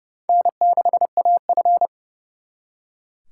Morse code, heard as a keyed tone, about 36 wpm, 700 Hz. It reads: N6AF
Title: Speed Racing - USA 1x2 Call Signs 30wpm